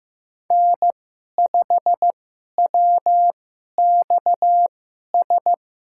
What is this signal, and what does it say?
Morse code, heard as a keyed tone, about 15 wpm, spaced as for 11 wpm, 700 Hz. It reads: N5WXS
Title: USA Call Signs - 5 Characters - 2x Character Spacing - 15wpm